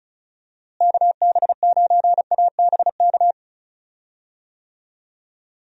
Morse code, heard as a keyed tone, about 34 wpm, 700 Hz. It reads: KB9ABK
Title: USA Call Signs - 6 Characters - Repeated 3x - 35wpm